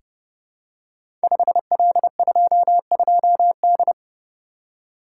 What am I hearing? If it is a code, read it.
Morse code, heard as a keyed tone, about 28 wpm, 700 Hz. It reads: 5L22D
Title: Code Groups - 5 Letter:Numbers - 30wpm